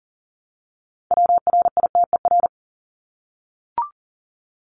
Morse code, heard as a keyed tone, about 40 wpm, 700 Hz. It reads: WRITER
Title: Top 1751-2000 Words 40wpm